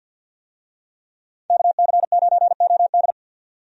Morse code, heard as a keyed tone, about 49 wpm, 700 Hz. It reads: KC9OD
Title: USA Call Signs - 5 Characters - 50wpm